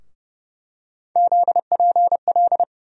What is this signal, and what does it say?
Morse code, heard as a keyed tone, about 30 wpm, 700 Hz. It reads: ZPL
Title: USA Call Sign Postfixes - 3 Letters 30wpm